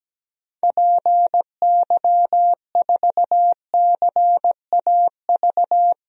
Morse code, heard as a keyed tone, about 17 wpm, 700 Hz. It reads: PY4CAV
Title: DXCC-100 Call Signs - 6 Characters - 17wpm